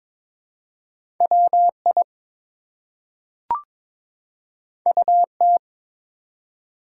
Morse code, heard as a keyed tone, about 22 wpm, 700 Hz. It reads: WI UT
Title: US States and Canadian Provinces 22wpm